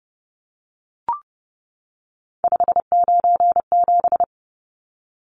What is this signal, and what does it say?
Morse code, heard as a keyed tone, about 30 wpm, 700 Hz. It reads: 597
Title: Numbers - 3-Digit 30wpm